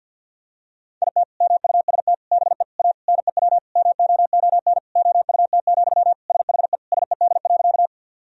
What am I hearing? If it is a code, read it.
Morse code, heard as a keyed tone, about 51 wpm, 700 Hz. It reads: IT MUST BE A NEW MOON OUT, SHE SED.
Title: Sentences from Top 500 Words and Abbreviations 50wpm